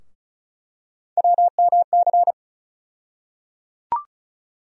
Morse code, heard as a keyed tone, about 36 wpm, 700 Hz. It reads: WMC
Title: USA Call Sign Postfixes - 3 Letters 35wpm